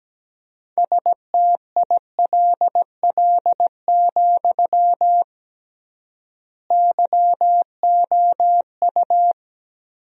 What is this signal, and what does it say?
Morse code, heard as a keyed tone, about 17 wpm, 700 Hz. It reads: STILL, YOU
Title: 3x Word-Spacing - Sentences from Top 300 Words - 17wpm